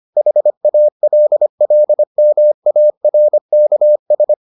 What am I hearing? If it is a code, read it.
Morse code, heard as a keyed tone, about 25 wpm, 600 Hz. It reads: HALLMARKS